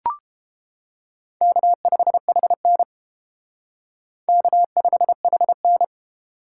Morse code, heard as a keyed tone, about 30 wpm, 700 Hz. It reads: K5HN K5HN